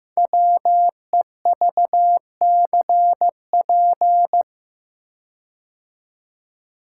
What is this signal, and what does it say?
Morse code, heard as a keyed tone, about 15 wpm, 700 Hz. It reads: WEVCP
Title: Code Groups - 5 Letter:Numbers - 15wpm